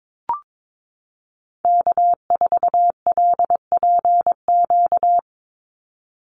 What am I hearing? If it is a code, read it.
Morse code, heard as a keyed tone, about 22 wpm, 700 Hz. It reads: K4LPQ
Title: NA SOTA Activator Callsigns 22wpm